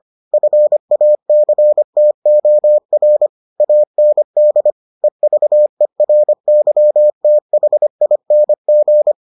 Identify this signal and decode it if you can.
Morse code, heard as a keyed tone, about 25 wpm, 600 Hz. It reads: FACTOR AND EVERYTHING